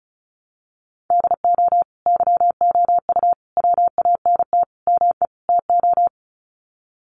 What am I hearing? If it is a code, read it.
Morse code, heard as a keyed tone, about 34 wpm, 700 Hz. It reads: DO YOU WANT ME TO
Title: Sets of 5 Words 35wpm